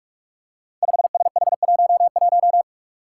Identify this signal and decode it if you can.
Morse code, heard as a keyed tone, about 43 wpm, 700 Hz. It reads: 5SH11
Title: Five Challenging Letter-Number Combinations 45wpm